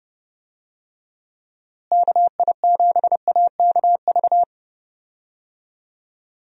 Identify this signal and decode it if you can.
Morse code, heard as a keyed tone, about 30 wpm, 700 Hz. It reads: KI7AKV
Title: USA Call Signs - 6 Characters - Repeated 2x - 30wpm